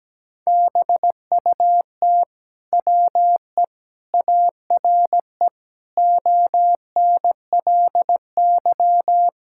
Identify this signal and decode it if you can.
Morse code, heard as a keyed tone, about 17 wpm, 700 Hz. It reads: BUT WE ARE ONLY
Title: Sentences from Top 1000 Words - Review Part 5 - 17wpm